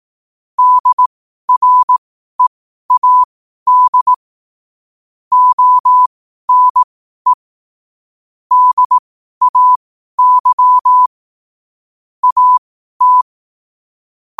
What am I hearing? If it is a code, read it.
Morse code, heard as a keyed tone, about 18 wpm, spaced as for 12 wpm, 1000 Hz. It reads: DREAD ONE DAY AT
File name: qotdfiles / 12wpm